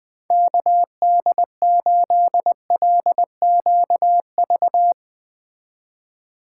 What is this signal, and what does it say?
Morse code, heard as a keyed tone, about 21 wpm, 700 Hz. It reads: KD8LQV